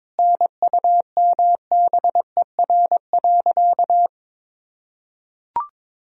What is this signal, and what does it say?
Morse code, heard as a keyed tone, about 22 wpm, 700 Hz. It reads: NUMBER.
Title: Sentences from Top 1000 Words - Review Part 1 - 22wpm